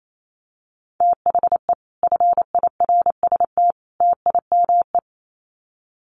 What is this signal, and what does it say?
Morse code, heard as a keyed tone, about 28 wpm, 700 Hz. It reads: THE FIRST TIME
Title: Sets of 3 Words 28wpm